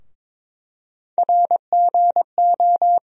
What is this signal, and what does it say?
Morse code, heard as a keyed tone, about 23 wpm, 700 Hz. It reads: RGO
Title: USA Call Sign Postfixes - 3 Letters 22wpm